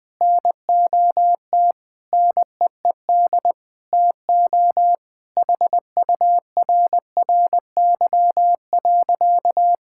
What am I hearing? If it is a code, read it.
Morse code, heard as a keyed tone, about 20 wpm, 700 Hz. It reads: NOT NEED TO HURRY.
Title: Sentences with 5-Word Phrases - 20wpm